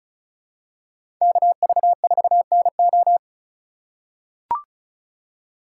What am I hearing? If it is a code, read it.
Morse code, heard as a keyed tone, about 36 wpm, 700 Hz. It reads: KV4NO
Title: USA Call Signs - 5 Characters - 4x Character Spacing - 35wpm